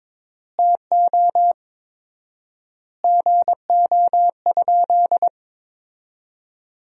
Morse code, heard as a keyed tone, about 22 wpm, 700 Hz. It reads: TO GO?
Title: 4x Word-Spacing - Sentences from Top 300 Words - 22wpm